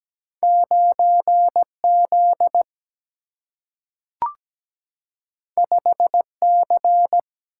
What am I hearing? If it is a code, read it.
Morse code, heard as a keyed tone, about 17 wpm, 700 Hz. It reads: 9 Z 5 C
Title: Single-Letter Single-Number 17wpm